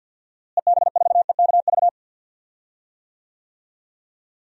Morse code, heard as a keyed tone, about 49 wpm, 700 Hz. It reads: EB4EKV
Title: DXCC-100 Call Signs - 6 Characters - Repeated 3x - 50wpm